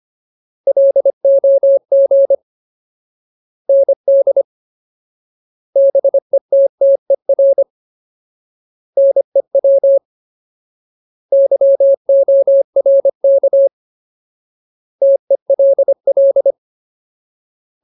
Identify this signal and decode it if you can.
Morse code, heard as a keyed tone, about 25 wpm, 550 Hz. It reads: LOG ND BETTER NEW YORK TELL
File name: Ultimate Word List / 25WPM Ultimate